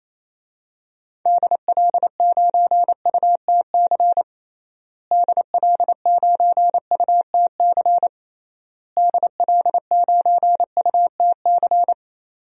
Morse code, heard as a keyed tone, about 28 wpm, 700 Hz. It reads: DL9UTC DL9UTC DL9UTC
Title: DXCC-100 Call Signs - 6 Characters - Repeated 3x - 28wpm